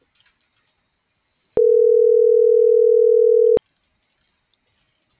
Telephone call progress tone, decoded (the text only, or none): ringback tone